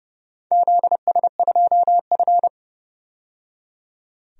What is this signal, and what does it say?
Morse code, heard as a keyed tone, about 30 wpm, 700 Hz. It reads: ZS2F
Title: DXCC-100 Call Signs - 4 Characters - 30wpm